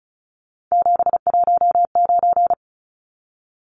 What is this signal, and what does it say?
Morse code, heard as a keyed tone, about 34 wpm, 700 Hz. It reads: 719
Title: Numbers - 3-Digit 35wpm